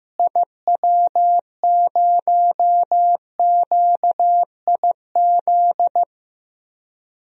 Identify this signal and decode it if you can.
Morse code, heard as a keyed tone, about 15 wpm, 700 Hz. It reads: IW0QIZ
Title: DXCC-100 Call Signs - 6 Characters - Repeated 3x - 15wpm